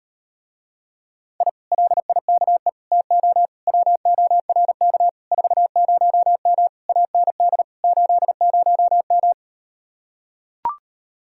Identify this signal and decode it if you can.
Morse code, heard as a keyed tone, about 38 wpm, 700 Hz. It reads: I LIKE TO WORK 40M AND 80M